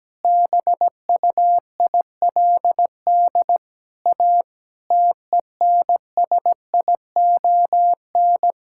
Morse code, heard as a keyed tone, about 17 wpm, 700 Hz. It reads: BUILD A TENSION
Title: Sentences from Top 1600 Words 17wpm